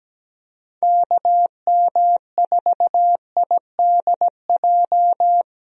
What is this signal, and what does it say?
Morse code, heard as a keyed tone, about 17 wpm, 700 Hz. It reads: KM4IDJ
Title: USA Call Signs - 6 Characters - Repeated 3x - 17wpm